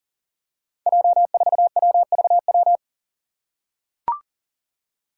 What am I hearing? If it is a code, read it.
Morse code, heard as a keyed tone, about 40 wpm, 700 Hz. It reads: J4WVW
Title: Five Challenging Letter-Number Combinations 40wpm